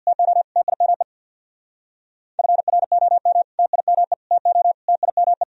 Morse code, heard as a keyed tone, about 59 wpm, 700 Hz. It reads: TO TIME FROM TIME TO TIME
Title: Speed-Racing - Sets of 4 Words 50wpm